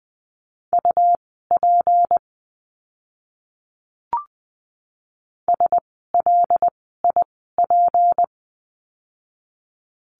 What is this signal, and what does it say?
Morse code, heard as a keyed tone, about 20 wpm, spaced as for 14 wpm, 700 Hz. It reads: UP SLIP
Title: 153 Lesson 25f - Letter p - Words 20wpm